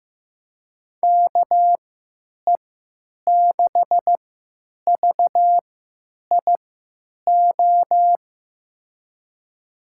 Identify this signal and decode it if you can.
Morse code, heard as a keyed tone, about 15 wpm, spaced as for 9 wpm, 700 Hz. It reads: KE6VIO
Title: USA Call Signs - 6 Characters - 3x Character Spacing - 15wpm